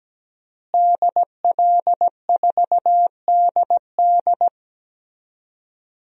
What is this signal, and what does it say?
Morse code, heard as a keyed tone, about 17 wpm, 700 Hz. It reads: DL4DD
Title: DXCC-100 Call Signs - 5 Characters - Repeated 2x - 17wpm